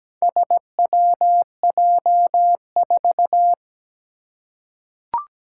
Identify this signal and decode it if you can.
Morse code, heard as a keyed tone, about 17 wpm, 700 Hz. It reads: SWJ4
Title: Four Challenging Letter-Number Combinations 17wpm